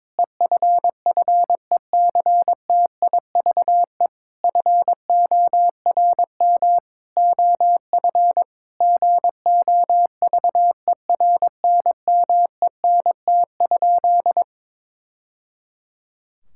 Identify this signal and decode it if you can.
Morse code, heard as a keyed tone, about 22 wpm, 700 Hz. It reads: EFFECTIVE FORM OF GOVERNMENT?
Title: Sentences from Top 1000 Words - Review Part 6 - 22wpm